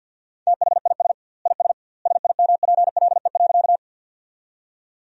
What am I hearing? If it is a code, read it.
Morse code, heard as a keyed tone, about 50 wpm, 700 Hz. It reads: THIS IS SIMPLE.
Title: 2x Word-Spacing - Sentences from Top 400 Words - 50wpm